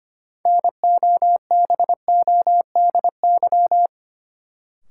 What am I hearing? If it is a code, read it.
Morse code, heard as a keyed tone, about 25 wpm, 700 Hz. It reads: NOBODY
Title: Sentences from Top 1600 Words - New Word Drill Alt - 25wpm